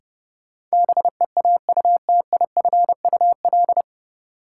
Morse code, heard as a keyed tone, about 30 wpm, 700 Hz. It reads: BEAUTIFUL
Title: Sentences from Top 900 Words - New Word Drill - 30wpm